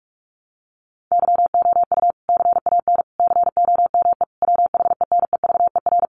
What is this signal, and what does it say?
Morse code, heard as a keyed tone, about 44 wpm, 700 Hz. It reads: YOU CAN COME WHENEVER
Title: Sentences from Top 1200 Words 45wpm